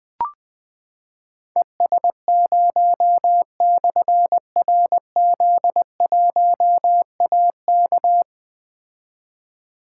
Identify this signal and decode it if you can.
Morse code, heard as a keyed tone, about 20 wpm, 700 Hz. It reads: ES0/RZ1AK
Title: Difficult Call Signs 20wpm